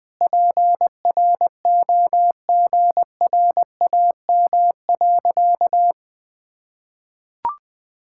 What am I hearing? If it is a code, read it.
Morse code, heard as a keyed tone, about 21 wpm, 700 Hz. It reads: PROGRAM.
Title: Sentences from Top 1000 Words - Review Part 5 - 20wpm